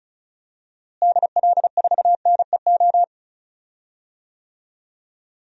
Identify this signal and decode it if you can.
Morse code, heard as a keyed tone, about 36 wpm, 700 Hz. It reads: DL4NEO
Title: DXCC-100 Call Signs - 6 Characters - Repeated 2x - 35wpm